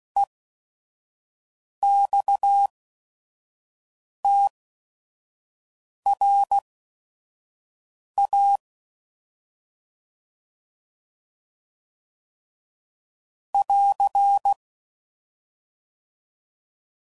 Morse code, heard as a keyed tone, about 16 wpm, spaced as for 5 wpm, 800 Hz. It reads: EXTRA <AR>
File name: CW Code Groups @5WPM / fromSM's / Classes